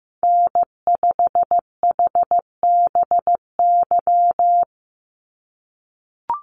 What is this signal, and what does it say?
Morse code, heard as a keyed tone, about 15 wpm, 700 Hz. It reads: N5HBY
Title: USA Call Signs 15wpm